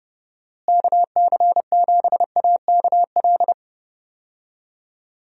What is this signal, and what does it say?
Morse code, heard as a keyed tone, about 30 wpm, 700 Hz. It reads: KC7AKL
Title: USA Call Signs - 6 Characters - Repeated 2x - 30wpm